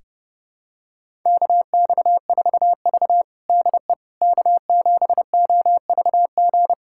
Morse code, heard as a keyed tone, about 30 wpm, 700 Hz. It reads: KX4V DE K7OVG
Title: USA Callsign DE Callsign 30wpm